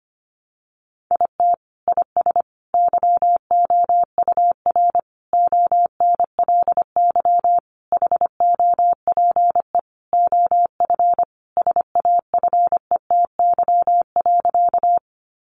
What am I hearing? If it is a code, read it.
Morse code, heard as a keyed tone, about 25 wpm, 700 Hz. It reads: IT IS YOUR ONLY HOPE OF SAFETY.
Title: Sentences from Top 1000 Words 25wpm